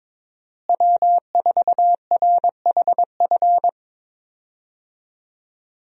Morse code, heard as a keyed tone, about 22 wpm, 700 Hz. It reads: W4RHF
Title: USA Call Signs - 5 Characters - Repeated 3x - 22wpm